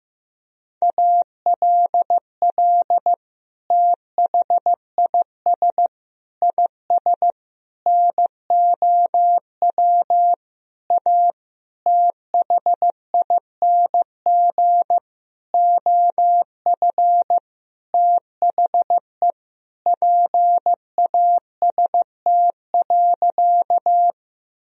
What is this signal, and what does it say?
Morse code, heard as a keyed tone, about 15 wpm, 700 Hz. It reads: ALL THIS IS NOW A THING OF THE PAST.
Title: Sentences from Top 1000 Words - Review Part 3 - 15wpm